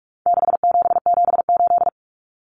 Morse code, heard as a keyed tone, about 44 wpm, 700 Hz. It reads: 6778
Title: Numbers - 4-Digit 45wpm